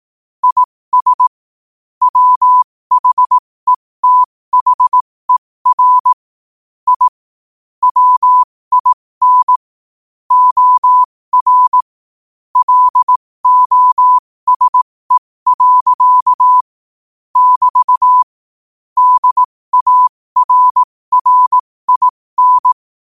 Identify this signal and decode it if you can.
Morse code, heard as a keyed tone, about 18 wpm, spaced as for 15 wpm, 1000 Hz. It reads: IS WHETHER I WIN OR LOSE. <BT> DARRIN
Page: 15 WPM morse code quotes for Tue, 23 Sep 2025.